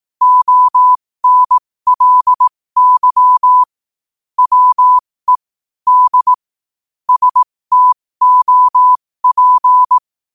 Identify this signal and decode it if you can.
Morse code, heard as a keyed tone, about 18 wpm, spaced as for 15 wpm, 1000 Hz. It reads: ONLY WE D STOP
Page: 15 WPM morse code quotes